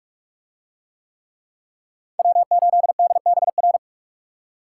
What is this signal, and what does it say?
Morse code, heard as a keyed tone, about 40 wpm, 700 Hz. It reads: W8DBR